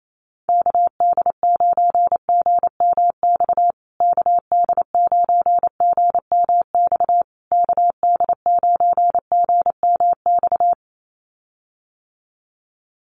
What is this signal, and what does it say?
Morse code, heard as a keyed tone, about 28 wpm, 700 Hz. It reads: KD9GMX KD9GMX KD9GMX
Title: USA Call Signs Repeated 3x 28wpm